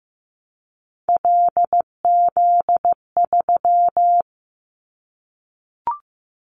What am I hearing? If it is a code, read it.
Morse code, heard as a keyed tone, about 15 wpm, 700 Hz. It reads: LZ3
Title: Callsign Prefixes for Top 25 DXCC - 15wpm